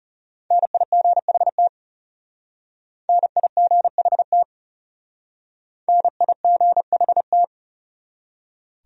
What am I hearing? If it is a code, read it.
Morse code, heard as a keyed tone, about 34 wpm, 700 Hz. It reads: NIGHT NIGHT NIGHT
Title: Speed-Racing Top 300 Words 30wpm